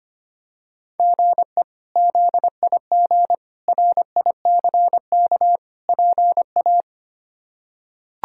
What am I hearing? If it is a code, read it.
Morse code, heard as a keyed tone, about 25 wpm, 700 Hz. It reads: GE ZIG RICK PA
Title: K1USN SST QSO Practice - 25wpm